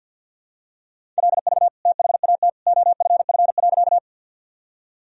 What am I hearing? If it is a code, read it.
Morse code, heard as a keyed tone, about 52 wpm, 700 Hz. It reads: LV THAT OFF.
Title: Sentences from Top 400 Words and Abbreviations 50wpm